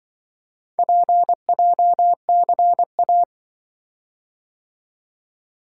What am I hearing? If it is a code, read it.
Morse code, heard as a keyed tone, about 25 wpm, 700 Hz. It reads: PJCA
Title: Speed-Racing - Four Letter-Number Combinations - 20wpm